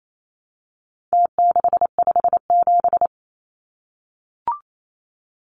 Morse code, heard as a keyed tone, about 28 wpm, 700 Hz. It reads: T657
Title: Numbers - Contest Style - 4-Digit 28wpm